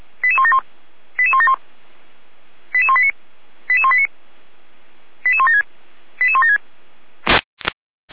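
Five-tone selective call (ZVEI), heard as quick runs of five tones, tones of about 70 ms. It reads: 88171 88171 88189 88189 88176 88176
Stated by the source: ZVEI five-tone selective call received on an SDR.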